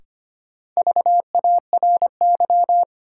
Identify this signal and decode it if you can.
Morse code, heard as a keyed tone, about 25 wpm, 700 Hz. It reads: VARY
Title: Sentences from Top 1200 Words - New Word Drill Alt - 25wpm